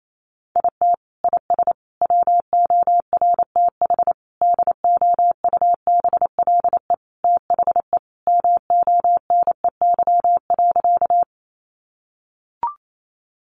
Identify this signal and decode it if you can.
Morse code, heard as a keyed tone, about 28 wpm, 700 Hz. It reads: IT IS WORTH DOUBLE THE MONEY.
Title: Sentences from Top 1400 Words 28wpm